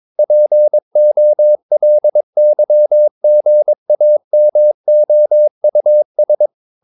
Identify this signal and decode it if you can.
Morse code, heard as a keyed tone, about 22 wpm, 600 Hz. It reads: POLYGAMOUS
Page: Romanen Never af Ken Follet på engelsk.